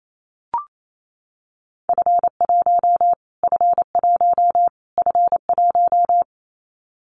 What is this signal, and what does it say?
Morse code, heard as a keyed tone, about 28 wpm, 700 Hz. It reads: F1 F1 F1
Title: Callsign Prefixes for Top 25 DXCC - Repeated 3x - 28wpm